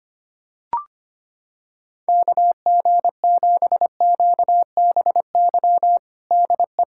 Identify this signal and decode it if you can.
Morse code, heard as a keyed tone, about 25 wpm, 700 Hz. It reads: KG7QBY DE
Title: USA Callsign DE Callsign - 6 Characters - 25wpm